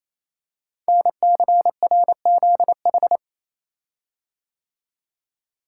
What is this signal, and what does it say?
Morse code, heard as a keyed tone, about 28 wpm, 700 Hz. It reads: NCRZH
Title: Code Groups - 5 Letters - 28wpm